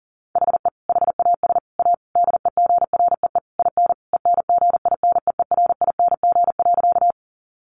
Morse code, heard as a keyed tone, about 40 wpm, 700 Hz. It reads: HE HAS A DEGREE IN ENGINEERING.
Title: Sentences from Top 1500 Words 40wpm